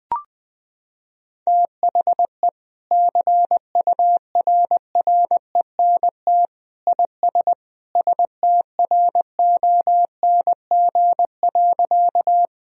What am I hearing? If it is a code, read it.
Morse code, heard as a keyed tone, about 20 wpm, 700 Hz. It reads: THE CURRENT IS STRONG.
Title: Sentences from Top 1000 Words - Review Part 4 - 20wpm